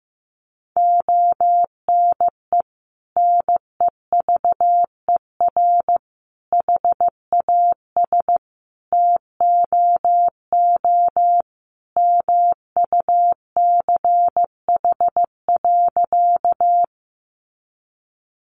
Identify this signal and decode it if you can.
Morse code, heard as a keyed tone, about 15 wpm, 700 Hz. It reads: ONE NEVER HAS TOO MUCH.
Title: Sentences from Top 300 Words 15wpm